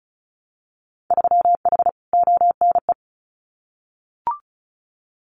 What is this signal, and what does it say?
Morse code, heard as a keyed tone, about 36 wpm, 700 Hz. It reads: 3H ONE
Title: Winter Field Day Exchanges Encore 35wpm